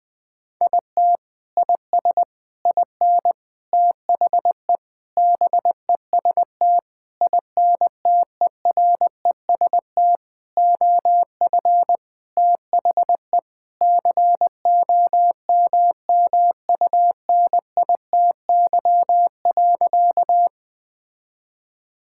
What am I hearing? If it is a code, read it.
Morse code, heard as a keyed tone, about 20 wpm, 700 Hz. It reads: IT IS IN THE BEST INTEREST OF THE COMMUNITY.
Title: Sentences with 6-Word Phrases - 20wpm